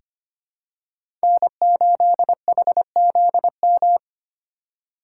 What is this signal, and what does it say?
Morse code, heard as a keyed tone, about 25 wpm, 700 Hz. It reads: N8HZM
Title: K1USN SST QSO Practice - 25wpm